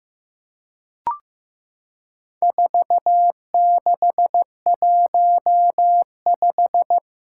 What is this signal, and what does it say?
Morse code, heard as a keyed tone, about 15 wpm, 700 Hz. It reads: 4615
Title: Four Challenging Letter-Number Combinations 15wpm